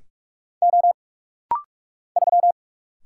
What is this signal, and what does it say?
Morse code, heard as a keyed tone, about 45 wpm, 700 Hz. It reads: O 3
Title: Single Letter-Number - Rapid-Fire 45wpm